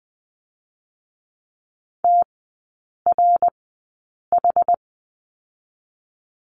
Morse code, heard as a keyed tone, about 20 wpm, spaced as for 8 wpm, 700 Hz. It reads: TRH
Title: 060 Lesson 11e - Letter h - Three Characters 20wpm